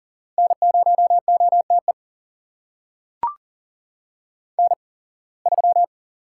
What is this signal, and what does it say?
Morse code, heard as a keyed tone, about 39 wpm, spaced as for 26 wpm, 700 Hz. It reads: N0OTE N3
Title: USA Call Signs - 5 Characters - 8x Character Spacing - 40wpm